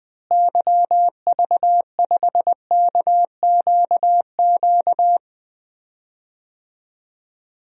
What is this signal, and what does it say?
Morse code, heard as a keyed tone, about 20 wpm, 700 Hz. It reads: YV5KQQ
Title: DXCC-100 Call Signs - 6 Characters - Repeated 3x - 20wpm